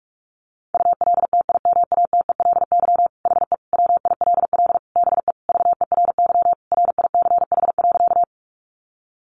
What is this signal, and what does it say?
Morse code, heard as a keyed tone, about 44 wpm, 700 Hz. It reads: ULTIMATELY HE WILL BE VERY RICH.
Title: Sentences from Top 1200 Words 45wpm